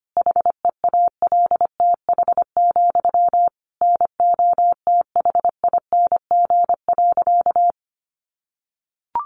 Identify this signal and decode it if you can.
Morse code, heard as a keyed tone, about 25 wpm, 700 Hz. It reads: HEALTH, NOTHING.
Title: Sentences from Top 900 Words 25wpm